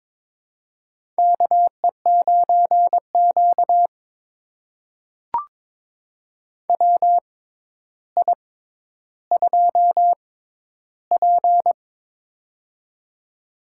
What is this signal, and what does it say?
Morse code, heard as a keyed tone, about 22 wpm, spaced as for 11 wpm, 700 Hz. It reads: KE9Q WI2P
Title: USA Call Signs - 4 Characters - 6x Character Spacing - 22wpm